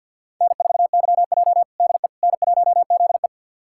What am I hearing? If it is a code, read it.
Morse code, heard as a keyed tone, about 48 wpm, 700 Hz. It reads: N4YJ DE N1ZE
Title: USA Callsign DE Callsign - 4 Characters - 50wpm